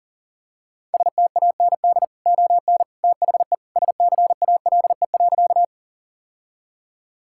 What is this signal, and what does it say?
Morse code, heard as a keyed tone, about 40 wpm, 700 Hz. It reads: STAND ON THE SCALE.
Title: Sentences from Top 1000 Words - Review Part 5 - 40wpm